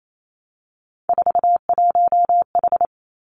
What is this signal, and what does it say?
Morse code, heard as a keyed tone, about 28 wpm, 700 Hz. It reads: 41H
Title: Three Challenging Letter-Number Combinations 28wpm